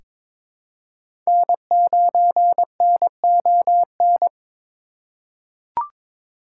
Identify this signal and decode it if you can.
Morse code, heard as a keyed tone, about 22 wpm, 700 Hz. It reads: N9NON
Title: USA Call Signs - 3x Character Spacing - 22wpm